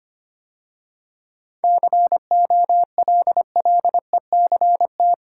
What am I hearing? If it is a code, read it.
Morse code, heard as a keyed tone, about 25 wpm, 700 Hz. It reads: COLLECT
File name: Easy Sentences with Hints - 25wpm